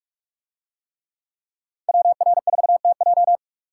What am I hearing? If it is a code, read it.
Morse code, heard as a keyed tone, about 42 wpm, 700 Hz. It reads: WR4TJ